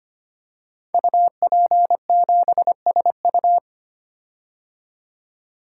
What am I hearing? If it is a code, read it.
Morse code, heard as a keyed tone, about 25 wpm, 700 Hz. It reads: UP7SU
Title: Code Groups - 5 Letter:Numbers - 25wpm